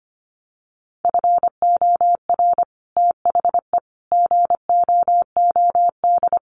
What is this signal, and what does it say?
Morse code, heard as a keyed tone, about 25 wpm, 700 Hz. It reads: FOR THE GOOD
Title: Sets of 5 Words Encore2 25wpm